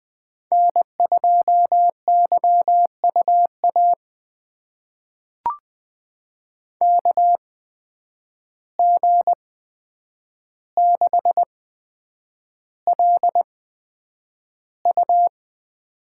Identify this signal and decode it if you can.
Morse code, heard as a keyed tone, about 20 wpm, spaced as for 9 wpm, 700 Hz. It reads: N2YUA KG6LU
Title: USA Call Signs - 8x Character Spacing - 20wpm